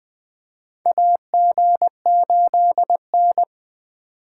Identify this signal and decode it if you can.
Morse code, heard as a keyed tone, about 20 wpm, 700 Hz. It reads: AG8N
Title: USA Call Signs - 4 Characters - 3x Character Spacing - 20wpm